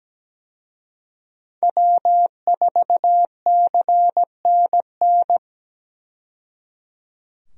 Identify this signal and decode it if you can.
Morse code, heard as a keyed tone, about 17 wpm, 700 Hz. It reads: W4CNN
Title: USA Call Signs - 5 Characters - 17wpm